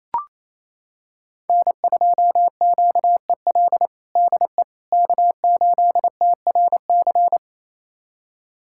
Morse code, heard as a keyed tone, about 28 wpm, 700 Hz. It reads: N2QEL DE K8TRC
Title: USA Callsign DE Callsign - 5 Characters - 28wpm